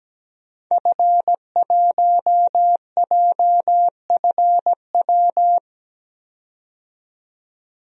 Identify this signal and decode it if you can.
Morse code, heard as a keyed tone, about 17 wpm, 700 Hz. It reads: F1JFW
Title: DXCC-100 Call Signs - 5 Characters - 17wpm